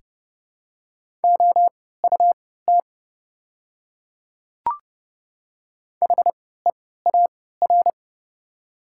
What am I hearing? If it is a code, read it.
Morse code, heard as a keyed tone, about 30 wpm, spaced as for 17 wpm, 690 Hz. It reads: OUT HEAR
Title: 3x Character Spacing - Top 300 Words - 30wpm